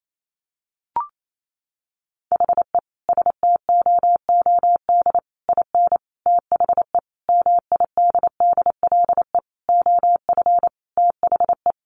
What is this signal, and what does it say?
Morse code, heard as a keyed tone, about 28 wpm, 700 Hz. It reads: HE STOOD IN THE MIDDLE OF THE
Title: Sentences from Top 1400 Words 28wpm